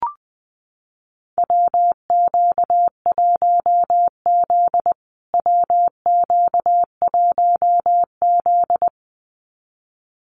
Morse code, heard as a keyed tone, about 20 wpm, 700 Hz. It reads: WQ1Z WQ1Z
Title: USA Call Signs Repeated 2x 20wpm